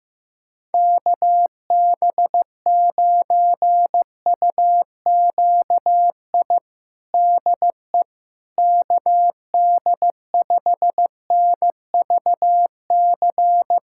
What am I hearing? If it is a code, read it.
Morse code, heard as a keyed tone, about 16 wpm, 700 Hz. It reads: KB9UQI DE KD5NVC
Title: USA Callsign DE Callsign - 6 Characters - 15wpm